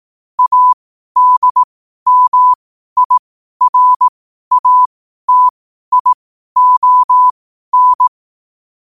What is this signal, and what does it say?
Morse code, heard as a keyed tone, about 18 wpm, spaced as for 12 wpm, 1000 Hz. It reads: ADMIRATION
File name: 12wpm